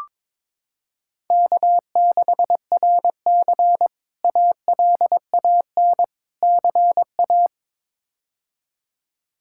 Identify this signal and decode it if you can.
Morse code, heard as a keyed tone, about 22 wpm, 700 Hz. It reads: K6RC ALAN CA
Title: K1USN SST QSO Practice - 22wpm